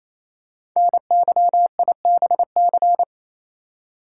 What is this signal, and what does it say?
Morse code, heard as a keyed tone, about 29 wpm, 700 Hz. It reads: NYIBC
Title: Code Groups - 5 Letter:Numbers - 28wpm